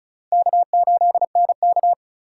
Morse code, heard as a keyed tone, about 34 wpm, 700 Hz. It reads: K8NK
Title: USA Call Signs - 4 Characters - 2x Character Spacing - 35wpm